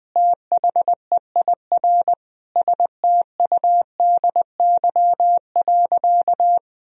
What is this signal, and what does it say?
Morse code, heard as a keyed tone, about 21 wpm, 700 Hz. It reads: THEIR STUDY.
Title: Sentences from Top 1000 Words - Review Part 3 - 20wpm